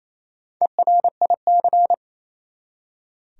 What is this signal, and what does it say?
Morse code, heard as a keyed tone, about 28 wpm, 700 Hz. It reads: ERIC
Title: USA Common Names - 28wpm